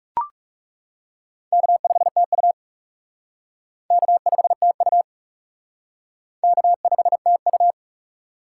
Morse code, heard as a keyed tone, about 40 wpm, 700 Hz. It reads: K5TU K5TU K5TU